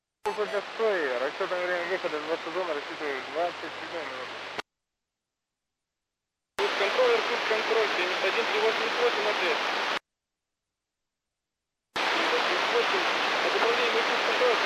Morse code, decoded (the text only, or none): none